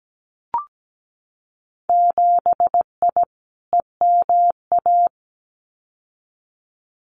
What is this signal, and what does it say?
Morse code, heard as a keyed tone, about 17 wpm, 700 Hz. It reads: 7I EMA
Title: Winter Field Day Exchanges Encore 17wpm